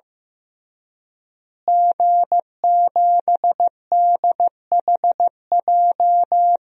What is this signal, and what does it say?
Morse code, heard as a keyed tone, about 15 wpm, 700 Hz. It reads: G7DHJ
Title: DXCC-100 Call Signs - 5 Characters - Repeated 3x - 15wpm